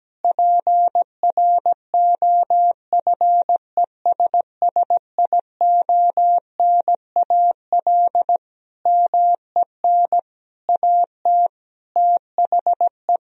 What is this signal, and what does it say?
Morse code, heard as a keyed tone, about 17 wpm, 700 Hz. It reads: PROFESSIONAL MEN AT THE
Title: Sentences from Top 1000 Words - Review Part 4 - 17wpm